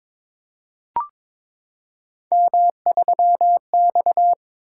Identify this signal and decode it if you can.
Morse code, heard as a keyed tone, about 22 wpm, 700 Hz. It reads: M3X
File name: practice / 22wpm / Three Letter-Number Combinations - 22wpm